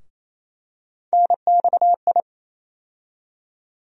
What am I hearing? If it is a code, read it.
Morse code, heard as a keyed tone, about 28 wpm, 700 Hz. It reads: NXI
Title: USA Call Sign Postfixes - 3 Letters 28wpm